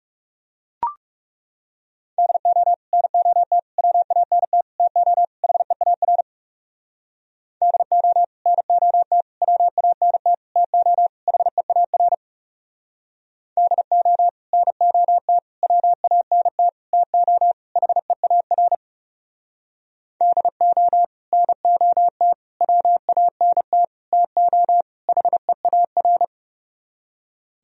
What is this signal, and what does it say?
Morse code, heard as a keyed tone, about 37 wpm, 700 Hz. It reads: DO NOT WANT TO HEAR DO NOT WANT TO HEAR DO NOT WANT TO HEAR DO NOT WANT TO HEAR
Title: Speed-Racing - Sets of 5 Words 30wpm